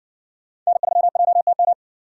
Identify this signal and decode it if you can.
Morse code, heard as a keyed tone, about 60 wpm, 700 Hz. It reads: N3JTM